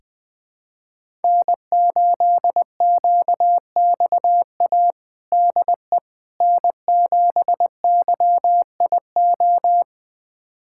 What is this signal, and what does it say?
Morse code, heard as a keyed tone, about 20 wpm, 700 Hz. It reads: N8QXA DE N7YIO
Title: USA Callsign DE Callsign - 5 Characters - 20wpm